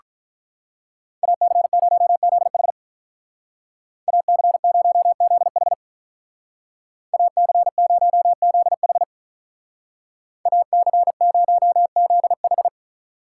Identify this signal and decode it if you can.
Morse code, heard as a keyed tone, about 45 wpm, 700 Hz. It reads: AC0ZH AC0ZH AC0ZH AC0ZH